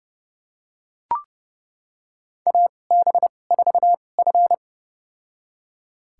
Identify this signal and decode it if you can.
Morse code, heard as a keyed tone, about 30 wpm, spaced as for 22 wpm, 700 Hz. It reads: AB4F
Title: USA Call Signs - 4 Characters - 2x Character Spacing - 30wpm